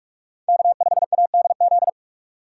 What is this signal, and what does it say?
Morse code, heard as a keyed tone, about 44 wpm, 700 Hz. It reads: K5ADZ
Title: USA Call Signs - 5 Characters - 6x Character Spacing - 45wpm